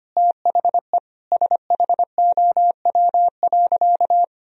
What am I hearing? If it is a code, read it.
Morse code, heard as a keyed tone, about 25 wpm, 700 Hz. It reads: THE SHOW.
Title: Sentences from Top 1000 Words - Review Part 2 - 25wpm